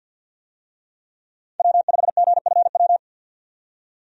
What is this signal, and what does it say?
Morse code, heard as a keyed tone, about 50 wpm, 700 Hz. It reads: W5GFW
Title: USA Call Signs - 5 Characters - 50wpm